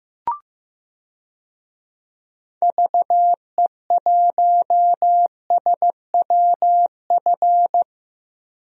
Morse code, heard as a keyed tone, about 15 wpm, 700 Hz. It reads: VE1SWF
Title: DXCC-100 Call Signs - 6 Characters - Repeated 2x - 15wpm